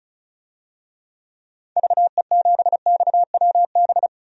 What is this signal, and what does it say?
Morse code, heard as a keyed tone, about 34 wpm, 700 Hz. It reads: VE7XWB
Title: DXCC-100 Call Signs - 6 Characters - 35wpm